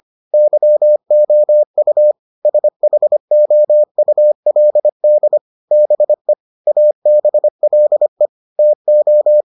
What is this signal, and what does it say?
Morse code, heard as a keyed tone, about 25 wpm, 600 Hz. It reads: YOU SHOULD BE ABLE TO